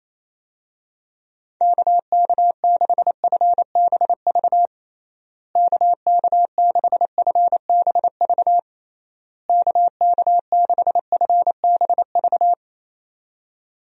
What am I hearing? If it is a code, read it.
Morse code, heard as a keyed tone, about 28 wpm, 700 Hz. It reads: KK6FBV KK6FBV KK6FBV
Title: USA Call Signs - 6 Characters - Repeated 3x - 28wpm